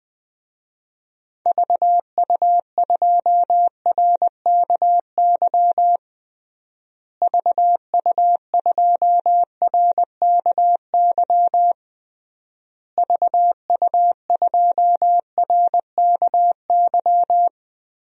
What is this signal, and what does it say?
Morse code, heard as a keyed tone, about 21 wpm, 700 Hz. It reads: VU2RKY VU2RKY VU2RKY
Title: DXCC-100 Call Signs - 6 Characters - Repeated 3x - 20wpm